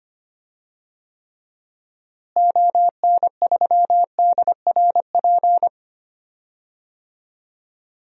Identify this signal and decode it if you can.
Morse code, heard as a keyed tone, about 25 wpm, 700 Hz. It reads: ON3DRP
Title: DXCC-100 Call Signs - 6 Characters - Repeated 2x - 25wpm